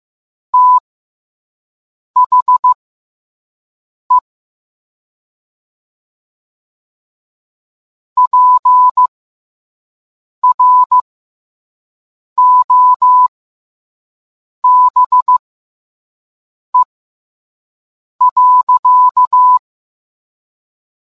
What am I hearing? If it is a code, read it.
Morse code, heard as a keyed tone, about 15 wpm, spaced as for 5 wpm, 1000 Hz. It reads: THE PROBE.